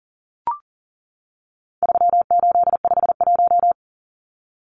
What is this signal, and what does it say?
Morse code, heard as a keyed tone, about 40 wpm, 700 Hz. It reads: 3851
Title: Numbers - 4-Digit 40wpm